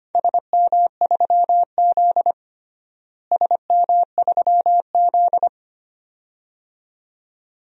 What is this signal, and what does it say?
Morse code, heard as a keyed tone, about 25 wpm, 700 Hz. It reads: SM3Z SM3Z
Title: DXCC-100 Call Signs - 4 Characters - Repeated 2x - 25wpm